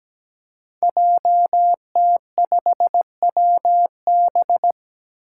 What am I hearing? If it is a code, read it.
Morse code, heard as a keyed tone, about 17 wpm, 700 Hz. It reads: JT5WB
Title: Five Challenging Letter-Number Combinations 17wpm